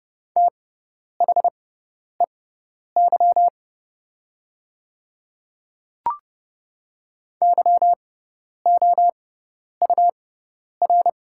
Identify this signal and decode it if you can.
Morse code, heard as a keyed tone, about 30 wpm, spaced as for 10 wpm, 700 Hz. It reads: THEY YOUR
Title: 6x Character Spacing - Top 200 Words - 30wpm